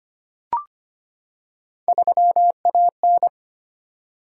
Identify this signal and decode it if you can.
Morse code, heard as a keyed tone, about 25 wpm, 700 Hz. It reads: 3AN
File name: Three Letter-Number Combinations - 25wpm